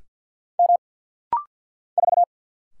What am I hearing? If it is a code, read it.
Morse code, heard as a keyed tone, about 49 wpm, 700 Hz. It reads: M 4
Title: Single Letter-Number - Rapid-Fire 50wpm